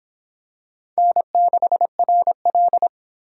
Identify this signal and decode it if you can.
Morse code, heard as a keyed tone, about 26 wpm, 700 Hz. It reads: N6RL